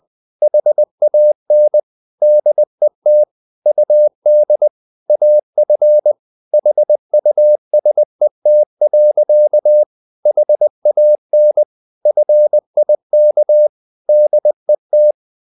Morse code, heard as a keyed tone, about 20 wpm, 600 Hz. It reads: HAN DET UD AF HUSET. HAN FIK DET
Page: Ferie 20 wpm